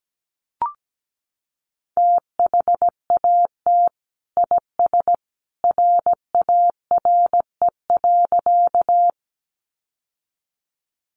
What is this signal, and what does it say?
Morse code, heard as a keyed tone, about 17 wpm, 700 Hz. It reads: THAT IS RARE.
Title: Sentences from Top 1100 Words 17wpm